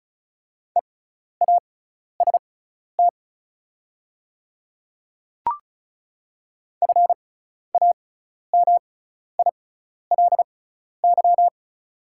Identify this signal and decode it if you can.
Morse code, heard as a keyed tone, about 36 wpm, spaced as for 12 wpm, 690 Hz. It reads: EAST FAMILY
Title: 6x Character Spacing - Top 500 Words - 35wpm